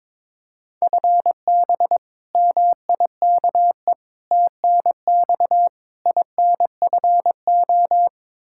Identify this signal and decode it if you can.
Morse code, heard as a keyed tone, about 21 wpm, 700 Hz. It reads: FB MIKE TNX INFO
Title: Practice QSO - Exchange 3 - More Info - 22wpm